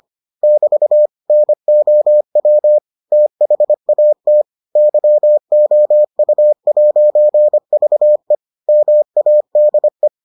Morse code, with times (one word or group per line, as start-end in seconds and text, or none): 0.4-1.1 <BT>
1.3-2.8 NOW
3.1-4.4 THAT
4.8-8.3 YOU'VE
8.7-10.1 MADE